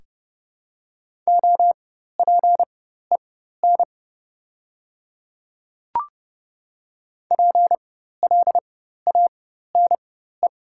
Morse code, heard as a keyed tone, about 30 wpm, spaced as for 14 wpm, 700 Hz. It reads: OPEN PLANE
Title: 4x Character Spacing - Top 500 Words - 30wpm